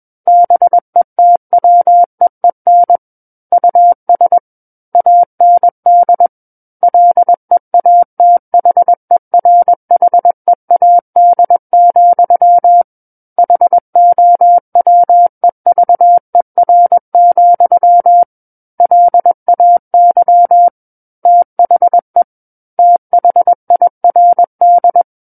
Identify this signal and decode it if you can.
Morse code, heard as a keyed tone, about 21 wpm, 700 Hz. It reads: BETWEEN US AND LEATHERHEAD, HOWEVER, LAY THE THIRD
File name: Book One / 12-Chapter 12 - 21 WPM